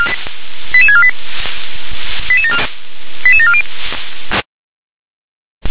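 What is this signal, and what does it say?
Five-tone selective call (ZVEI), heard as quick runs of five tones, tones of about 70 ms. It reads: 88648 88740
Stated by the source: ZVEI five-tone selective call received by SDR.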